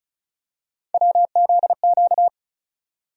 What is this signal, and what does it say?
Morse code, heard as a keyed tone, about 36 wpm, 700 Hz. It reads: WZQ
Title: USA Call Sign Postfixes - 3 Letters 35wpm